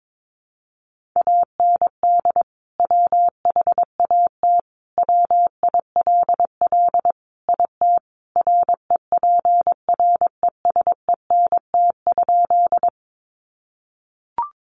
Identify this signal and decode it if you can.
Morse code, heard as a keyed tone, about 22 wpm, 700 Hz. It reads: AND WHAT WILL IT REPRESENT?
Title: Sentences from Top 1200 Words 22wpm